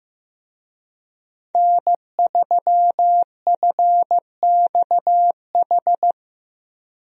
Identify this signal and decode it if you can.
Morse code, heard as a keyed tone, about 14 wpm, 700 Hz. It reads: N3FXH
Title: USA Call Signs - 5 Characters - Repeated 2x - 15wpm